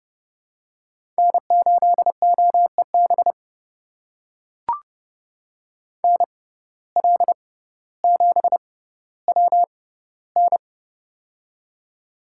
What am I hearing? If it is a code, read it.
Morse code, heard as a keyed tone, about 30 wpm, spaced as for 15 wpm, 700 Hz. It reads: N8OEB NL7WN
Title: USA Call Signs - 5 Characters - 6x Character Spacing - 30wpm